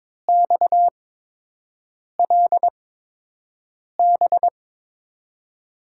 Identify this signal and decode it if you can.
Morse code, heard as a keyed tone, about 21 wpm, spaced as for 6 wpm, 700 Hz. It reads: XLB
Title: USA Call Signs - 8x Character Spacing - 22wpm